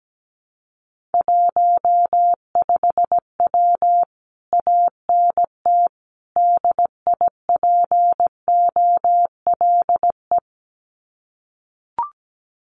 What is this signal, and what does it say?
Morse code, heard as a keyed tone, about 17 wpm, 700 Hz. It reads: 15W ANT DIPOLE
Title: Common QSO Phrases Encore 17wpm